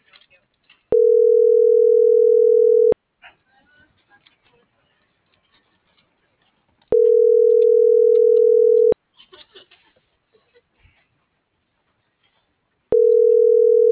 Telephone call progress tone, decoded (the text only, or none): ringback tone